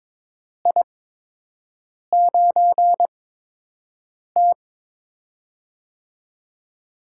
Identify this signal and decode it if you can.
Morse code, heard as a keyed tone, about 23 wpm, spaced as for 6 wpm, 700 Hz. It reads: I9T
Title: USA Call Signs - 4 Characters - 8x Character Spacing - 22wpm